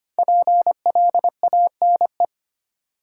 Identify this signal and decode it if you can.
Morse code, heard as a keyed tone, about 25 wpm, 700 Hz. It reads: PLANE
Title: Sentences from Top 1000 Words - Review Part 3 - New Word Drill - 25wpm